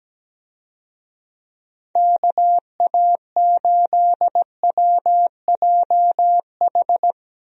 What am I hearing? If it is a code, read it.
Morse code, heard as a keyed tone, about 17 wpm, 700 Hz. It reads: KA8WJH
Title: USA Call Signs - 6 Characters - 17wpm